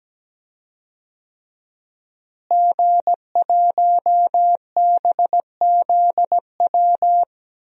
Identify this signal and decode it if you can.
Morse code, heard as a keyed tone, about 17 wpm, 700 Hz. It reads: G1BZW
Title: DXCC-100 Call Signs - 5 Characters - Repeated 3x - 17wpm